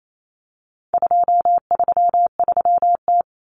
Morse code, heard as a keyed tone, about 28 wpm, 700 Hz. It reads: 233T
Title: Common Cut Numbers - 4-Digit 28wpm